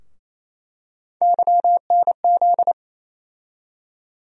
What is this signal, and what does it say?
Morse code, heard as a keyed tone, about 28 wpm, 700 Hz. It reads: YNZ
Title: USA Call Sign Postfixes - 3 Letters 28wpm